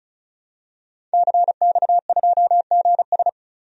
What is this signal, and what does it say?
Morse code, heard as a keyed tone, about 36 wpm, 700 Hz. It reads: CX2GS
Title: DXCC-100 Call Signs - 5 Characters - 35wpm